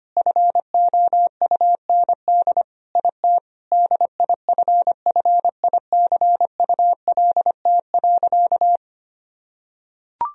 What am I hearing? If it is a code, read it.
Morse code, heard as a keyed tone, about 25 wpm, 700 Hz. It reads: FOUND IT DIFFICULT.
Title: Sentences from Top 1000 Words - Review Part 3 - 25wpm